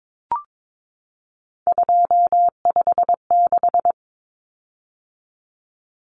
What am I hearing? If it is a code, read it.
Morse code, heard as a keyed tone, about 22 wpm, 700 Hz. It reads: 256
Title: Common Cut Numbers - 3-Digit 22wpm